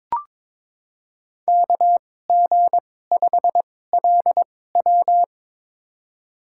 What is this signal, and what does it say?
Morse code, heard as a keyed tone, about 21 wpm, spaced as for 16 wpm, 700 Hz. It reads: KG5LW
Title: USA Call Signs - 5 Characters - 2x Character Spacing - 22wpm